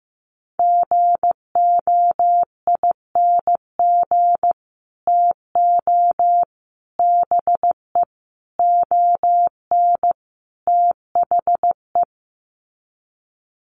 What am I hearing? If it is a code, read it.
Morse code, heard as a keyed tone, about 15 wpm, 700 Hz. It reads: GOING TO BE ON THE
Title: Sets of 5 Words Encore2 15wpm